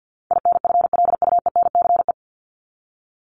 Morse code, heard as a keyed tone, about 50 wpm, 700 Hz. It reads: INFLUENCE
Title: Top 1001-1250 Words 50wpm